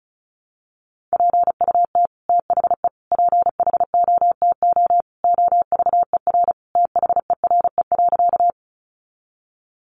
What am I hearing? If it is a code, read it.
Morse code, heard as a keyed tone, about 34 wpm, 700 Hz. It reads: PUT THE PHOTO OVER THERE.
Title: Sentences from Top 1200 Words 35wpm